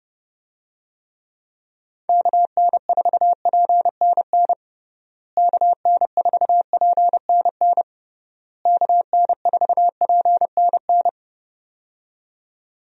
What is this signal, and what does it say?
Morse code, heard as a keyed tone, about 30 wpm, 700 Hz. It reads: KN4PNN KN4PNN KN4PNN
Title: USA Call Signs - 6 Characters - Repeated 3x - 30wpm